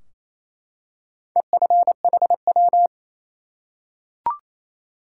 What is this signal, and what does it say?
Morse code, heard as a keyed tone, about 28 wpm, 700 Hz. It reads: EFHW
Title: Top 180 QSO Elements 28wpm